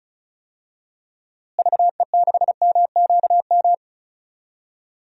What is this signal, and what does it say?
Morse code, heard as a keyed tone, about 36 wpm, 700 Hz. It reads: VE6MQM
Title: DXCC-100 Call Signs - 6 Characters - Repeated 2x - 35wpm